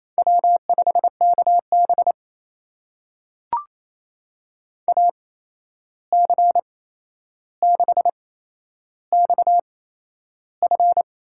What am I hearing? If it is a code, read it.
Morse code, heard as a keyed tone, about 28 wpm, spaced as for 11 wpm, 700 Hz. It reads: W5KB AC6XF
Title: USA Call Signs - 8x Character Spacing - 28wpm